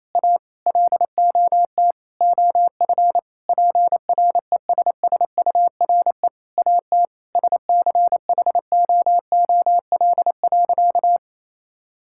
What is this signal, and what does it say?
Morse code, heard as a keyed tone, about 28 wpm, 700 Hz. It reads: A LOT OF PRESSURE AT SCHOOL.
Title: Sentences from Top 1000 Words - Review Part 6 - 28wpm